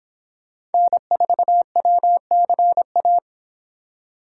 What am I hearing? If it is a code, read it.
Morse code, heard as a keyed tone, about 28 wpm, 700 Hz. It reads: N4WCA